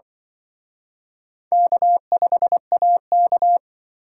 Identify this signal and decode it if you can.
Morse code, heard as a keyed tone, about 25 wpm, 700 Hz. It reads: K5AK
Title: Speed Racing - USA 1x2 Call Signs 20wpm